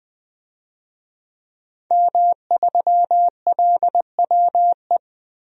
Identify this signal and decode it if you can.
Morse code, heard as a keyed tone, about 21 wpm, 700 Hz. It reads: M3LWE
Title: DXCC-100 Call Signs - 5 Characters - Repeated 2x - 20wpm